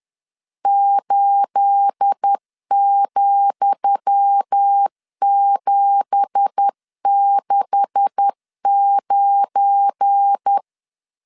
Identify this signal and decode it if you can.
Morse code, heard as a keyed tone, about 10 wpm, 800 Hz. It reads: 8,769
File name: LEZIONI / 10 SIGNS / 10D COMA 10wpm 3013